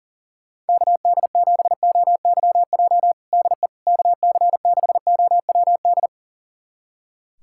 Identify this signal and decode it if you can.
Morse code, heard as a keyed tone, about 40 wpm, 700 Hz. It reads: KD7OYJ DE KC6OWD
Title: USA Callsign DE Callsign - 6 Characters - 40wpm